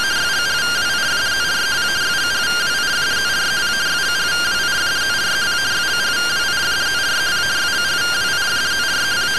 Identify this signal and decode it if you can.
Two-tone teleprinter signal, about 170 Hz shift.